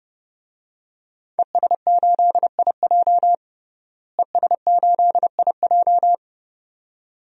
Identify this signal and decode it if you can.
Morse code, heard as a keyed tone, about 32 wpm, 700 Hz. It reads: ES8IJ ES8IJ
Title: DXCC-100 Call Signs - 5 Characters - Repeated 2x - 30wpm